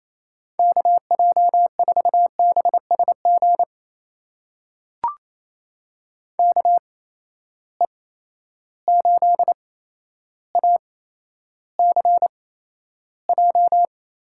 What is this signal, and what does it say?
Morse code, heard as a keyed tone, about 28 wpm, spaced as for 12 wpm, 700 Hz. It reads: KJ4BSG KE8ACJ
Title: USA Call Signs - 6 Characters - 8x Character Spacing - 28wpm